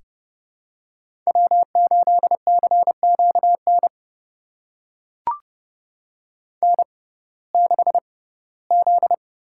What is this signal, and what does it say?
Morse code, heard as a keyed tone, about 30 wpm, spaced as for 18 wpm, 700 Hz. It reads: W8CQN N6Z
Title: USA Call Signs - 5 Characters - 6x Character Spacing - 30wpm